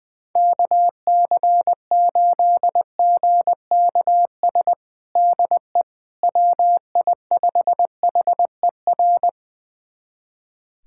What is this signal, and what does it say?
Morse code, heard as a keyed tone, about 20 wpm, 700 Hz. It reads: KC8GKS DE WI5HER
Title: USA Callsign DE Callsign 20wpm